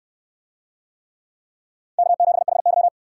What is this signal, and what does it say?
Morse code, heard as a keyed tone, about 64 wpm, 700 Hz. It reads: K7HY